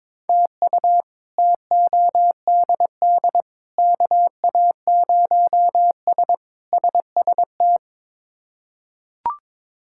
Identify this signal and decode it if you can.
Morse code, heard as a keyed tone, about 22 wpm, 700 Hz. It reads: TU TODD KA0S SST
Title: K1USN SST QSO Practice - 22wpm